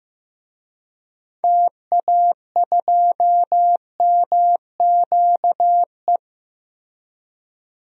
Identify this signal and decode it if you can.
Morse code, heard as a keyed tone, about 15 wpm, 700 Hz. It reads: TA2MQE
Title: DXCC-100 Call Signs - 6 Characters - 15wpm